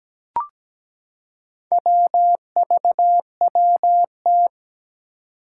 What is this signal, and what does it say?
Morse code, heard as a keyed tone, about 17 wpm, 700 Hz. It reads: WVWT
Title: Four Challenging Letter-Number Combinations 17wpm